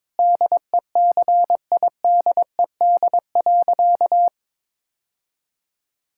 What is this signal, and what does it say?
Morse code, heard as a keyed tone, about 22 wpm, 700 Hz. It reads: DECIDED.
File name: practice / 22wpm / Sentences with 3-Word Phrases - 22wpm